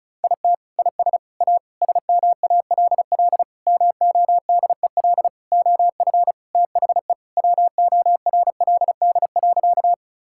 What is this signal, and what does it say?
Morse code, heard as a keyed tone, about 36 wpm, 700 Hz. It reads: IT IS A SMALL MODEL OF THE WORLD.
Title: Sentences from Top 1000 Words - Review Part 5 - 35wpm